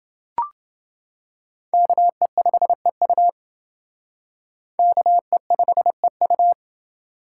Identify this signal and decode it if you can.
Morse code, heard as a keyed tone, about 29 wpm, 700 Hz. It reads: KE5EU KE5EU